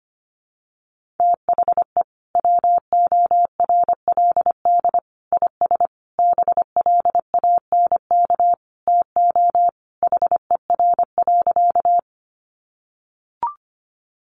Morse code, heard as a keyed tone, about 25 wpm, 700 Hz. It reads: THE WORLD IS BLANK TO HER.
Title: Sentences from Top 1500 Words 25wpm